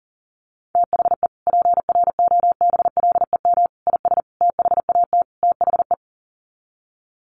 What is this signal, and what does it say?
Morse code, heard as a keyed tone, about 40 wpm, 700 Hz. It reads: THE PROBLEM IS THAT THE
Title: Sets of 5 Words Encore2 40wpm